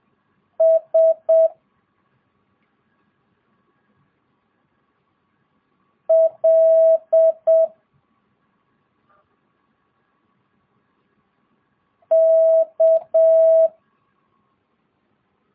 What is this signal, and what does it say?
Morse code, heard as a keyed tone, about 7 wpm, 650 Hz. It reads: S L K
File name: jh8cbh / 2022cw